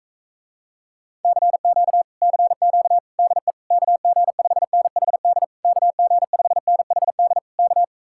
Morse code, heard as a keyed tone, about 45 wpm, 700 Hz. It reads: CQ CQ DE KG5NHD KG5NHD K